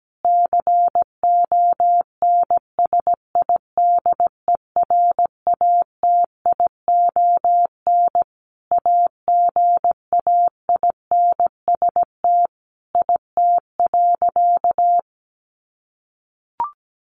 Morse code, heard as a keyed tone, about 17 wpm, 700 Hz. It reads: CONSIDERATION AGAINST IT.
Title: Sentences from Top 1500 Words 17wpm